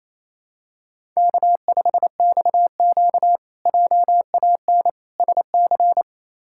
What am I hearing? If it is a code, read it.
Morse code, heard as a keyed tone, about 29 wpm, 700 Hz. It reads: K5XQ JAN SC
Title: K1USN SST QSO Practice - 28wpm